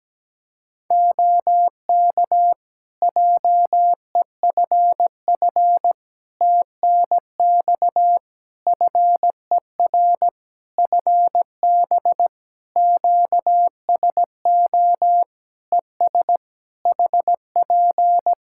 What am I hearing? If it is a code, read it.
Morse code, heard as a keyed tone, about 17 wpm, 700 Hz. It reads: OK JEFF TNX FER FB QSO ES HP
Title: K1USN SST QSO Practice 17wpm